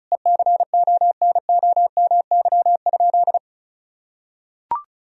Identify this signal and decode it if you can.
Morse code, heard as a keyed tone, about 36 wpm, 700 Hz. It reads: ECONOMY?
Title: Sentences from Top 1000 Words - Review Part 4 - 35wpm